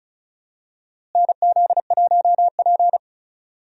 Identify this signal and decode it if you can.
Morse code, heard as a keyed tone, about 34 wpm, 700 Hz. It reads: NZ1P
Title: USA Call Signs - 4 Characters - Repeated 3x - 35wpm